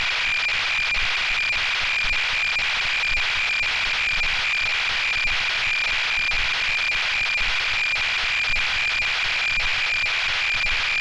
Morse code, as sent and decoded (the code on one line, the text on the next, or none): none